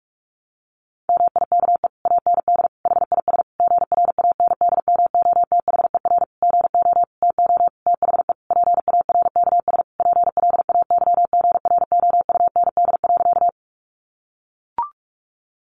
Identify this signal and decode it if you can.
Morse code, heard as a keyed tone, about 44 wpm, 700 Hz. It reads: MIKE AND HIS GRANDMOTHER GO TO THE PARKS PLAYGROUND.
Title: Additional Easy Phrases 45wpm